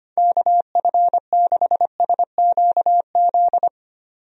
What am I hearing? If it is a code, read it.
Morse code, heard as a keyed tone, about 25 wpm, 700 Hz. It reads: KF6SQZ
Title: USA Call Signs - 6 Characters - 25wpm